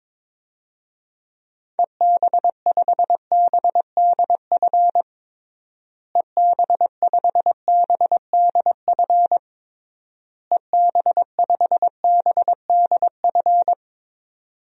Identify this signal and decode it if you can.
Morse code, heard as a keyed tone, about 21 wpm, 700 Hz. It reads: EB5BDF EB5BDF EB5BDF
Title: DXCC-100 Call Signs - 6 Characters - Repeated 3x - 22wpm